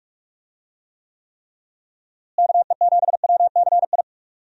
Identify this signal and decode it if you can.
Morse code, heard as a keyed tone, about 41 wpm, 700 Hz. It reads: KE7WCI
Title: USA Call Signs - 6 Characters - Repeated 3x - 45wpm